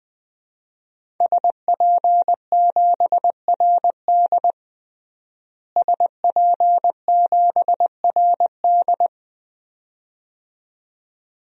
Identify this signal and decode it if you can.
Morse code, heard as a keyed tone, about 20 wpm, 700 Hz. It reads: SP7RD SP7RD
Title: DXCC-100 Call Signs - 5 Characters - Repeated 2x - 20wpm